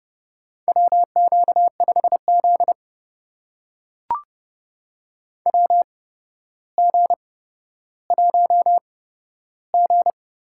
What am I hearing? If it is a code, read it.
Morse code, heard as a keyed tone, about 30 wpm, spaced as for 13 wpm, 700 Hz. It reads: WQ5Z WG1G
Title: USA Call Signs - 4 Characters - 8x Character Spacing - 30wpm